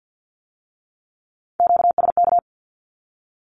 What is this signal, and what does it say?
Morse code, heard as a keyed tone, about 50 wpm, 700 Hz. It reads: QSK
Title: Q-Codes with Spoken Meaning - 50wpm